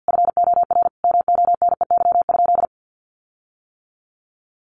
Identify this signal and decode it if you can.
Morse code, heard as a keyed tone, about 50 wpm, 700 Hz. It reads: FOR MONEY?
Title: Sentences from Top 1300 Words 50wpm